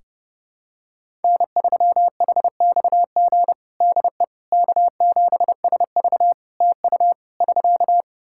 Morse code, heard as a keyed tone, about 30 wpm, 700 Hz. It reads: N3HXG DE K7SV TU <SK>
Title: K1USN SST QSO Practice 30wpm